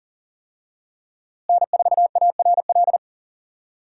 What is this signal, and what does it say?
Morse code, heard as a keyed tone, about 40 wpm, 700 Hz. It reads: N4ARL
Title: USA Call Signs - 5 Characters - Repeated 2x - 40wpm